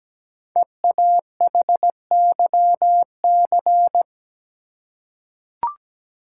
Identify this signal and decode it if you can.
Morse code, heard as a keyed tone, about 17 wpm, 700 Hz. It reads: EAHYC
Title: Code Groups - 5 Letters - 17wpm